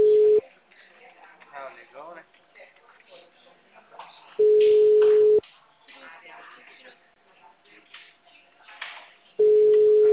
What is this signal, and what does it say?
Ringback tone (Europe, 425 Hz), with speech.